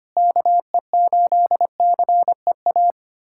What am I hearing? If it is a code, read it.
Morse code, heard as a keyed tone, about 25 wpm, 700 Hz. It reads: KE8CEA
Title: USA Callsign DE Callsign - 6 Characters - 25wpm